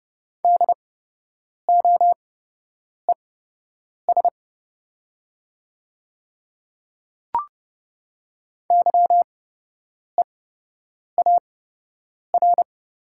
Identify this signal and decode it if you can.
Morse code, heard as a keyed tone, about 30 wpm, spaced as for 8 wpm, 700 Hz. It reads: DOES YEAR
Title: 8x Character Spacing - Top 500 Words - 30wpm